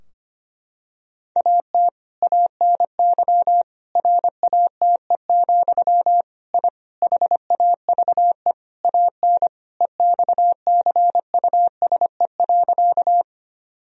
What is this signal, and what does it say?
Morse code, heard as a keyed tone, about 25 wpm, 700 Hz. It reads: AT ANY RATE, I HAVE AN EXCUSE.
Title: Sentences from Top 1000 Words - Review Part 4 - 25wpm